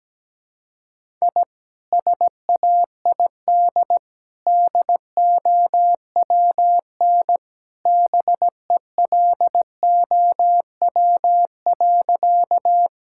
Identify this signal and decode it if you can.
Morse code, heard as a keyed tone, about 17 wpm, 700 Hz. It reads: I SAID DOWN BELOW.
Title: Sentences from Top 1000 Words - Review Part 5 - 17wpm